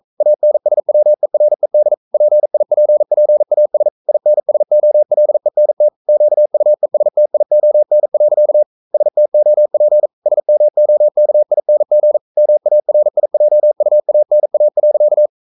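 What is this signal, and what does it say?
Morse code, heard as a keyed tone, about 42 wpm, 600 Hz. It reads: ANSWERED PIPPAS INSOLENT QUESTION. STOP SMOKING MARIJUANA.
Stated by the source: Morse code sending of an English novel at 42 wpm.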